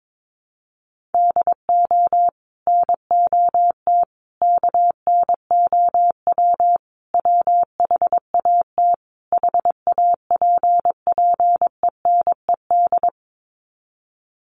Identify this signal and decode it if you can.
Morse code, heard as a keyed tone, about 22 wpm, 700 Hz. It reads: DO NOT KNOW WHAT HAPPENED
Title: Sets of 5 Words Encore2 22wpm